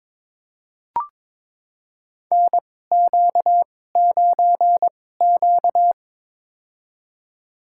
Morse code, heard as a keyed tone, about 22 wpm, spaced as for 16 wpm, 700 Hz. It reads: NQ9Q
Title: USA Call Signs - 4 Characters - 2x Character Spacing - 22wpm